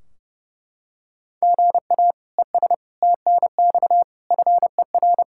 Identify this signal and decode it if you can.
Morse code, heard as a keyed tone, about 32 wpm, 700 Hz. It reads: GA ES TNX FER
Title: Practice QSO - Exchange 1 - Beginning - 30wpm